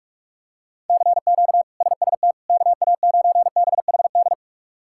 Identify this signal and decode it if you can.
Morse code, heard as a keyed tone, about 46 wpm, 700 Hz. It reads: CQ SST KA9BHD
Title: K1USN SST QSO Practice - 45wpm